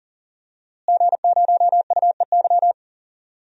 Morse code, heard as a keyed tone, about 37 wpm, 700 Hz. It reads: G0UEY
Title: DXCC-100 Call Signs - 5 Characters - Repeated 2x - 40wpm